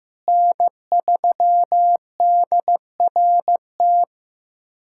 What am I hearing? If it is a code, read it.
Morse code, heard as a keyed tone, about 15 wpm, 700 Hz. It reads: N3DRT
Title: USA Callsign DE Callsign - 5 Characters - 15wpm